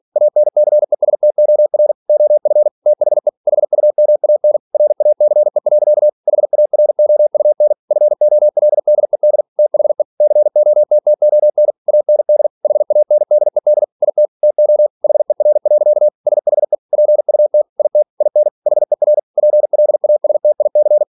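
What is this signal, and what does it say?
Morse code, heard as a keyed tone, about 46 wpm, 600 Hz. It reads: ANCESTOR OF THE HUMAN RACE. HAROUN FOLDED THE COTTON AND HANDED IT TO HER. SHE PUT IT IN HER PLASTIC
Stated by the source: Practice text from an English novel sent at 47 WPM.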